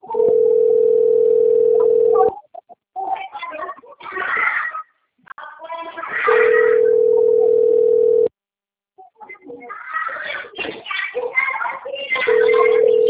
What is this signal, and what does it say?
Ringback tone (North America), 440 + 480 Hz, with speech.
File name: rg-1600-unknown- - - .85257